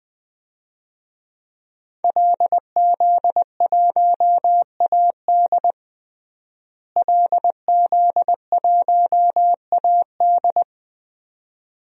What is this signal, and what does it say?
Morse code, heard as a keyed tone, about 20 wpm, 700 Hz. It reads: LZ1AD LZ1AD
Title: DXCC-100 Call Signs - 5 Characters - Repeated 2x - 20wpm